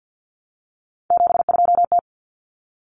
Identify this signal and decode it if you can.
Morse code, heard as a keyed tone, about 50 wpm, 700 Hz. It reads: 72T
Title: Common Cut Numbers - 3-Digit 50wpm